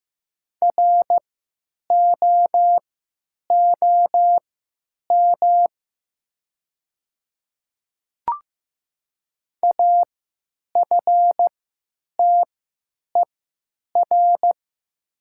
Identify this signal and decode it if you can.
Morse code, heard as a keyed tone, about 15 wpm, spaced as for 8 wpm, 700 Hz. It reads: ROOM AFTER
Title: 3x Character Spacing - Top 500 Words - 15wpm